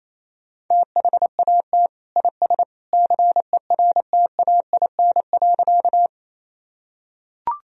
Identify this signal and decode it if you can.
Morse code, heard as a keyed tone, about 28 wpm, 700 Hz. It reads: THAT IS CERTAIN.
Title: Sentences from Top 1000 Words - Review Part 2 - 28wpm